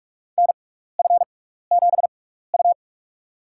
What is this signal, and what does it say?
Morse code, heard as a keyed tone, about 46 wpm, spaced as for 16 wpm, 700 Hz. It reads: NF7U
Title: USA Call Signs - 4 Characters - 6x Character Spacing - 45wpm